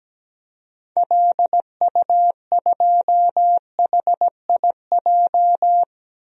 Morse code, heard as a keyed tone, about 17 wpm, 700 Hz. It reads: LU2HIJ
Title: DXCC-100 Call Signs - 6 Characters - 17wpm